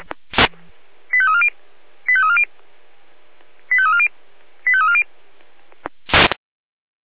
Five-tone selective call (ZVEI), heard as quick runs of five tones, tones of about 70 ms. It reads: 85340 85340 85340 85340